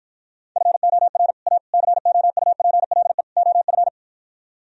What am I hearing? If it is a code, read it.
Morse code, heard as a keyed tone, about 52 wpm, 700 Hz. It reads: FOR A COUPLE OF